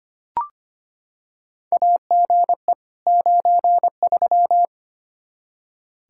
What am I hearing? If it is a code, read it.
Morse code, heard as a keyed tone, about 25 wpm, 700 Hz. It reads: AGE 93
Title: K1USN SST QSO Practice 25wpm